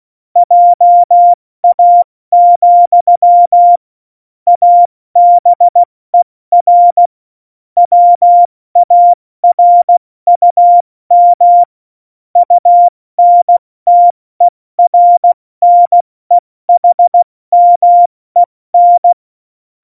Morse code, heard as a keyed tone, about 16 wpm, 700 Hz. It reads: JA, ABER WARUM UNTERNEHMEN
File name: Morsekurs / Witze / Witz 27. Oktober 2020, 70 F80